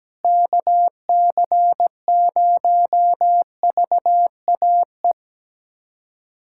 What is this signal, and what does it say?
Morse code, heard as a keyed tone, about 17 wpm, 700 Hz. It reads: KC0VAE
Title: USA Callsign DE Callsign - 6 Characters - 17wpm